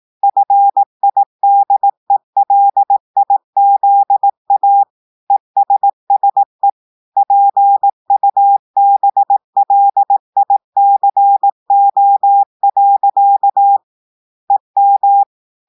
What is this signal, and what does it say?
Morse code, heard as a keyed tone, about 18 wpm, 800 Hz. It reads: FIDELIZA ESSE PUBLICO. EM
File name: audios / portugais / portugais texte en clair 2 vitesse 18 mots minutes